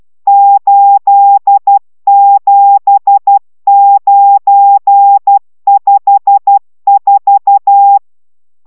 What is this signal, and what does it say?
Morse code, heard as a keyed tone, about 12 wpm, 800 Hz. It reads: 87954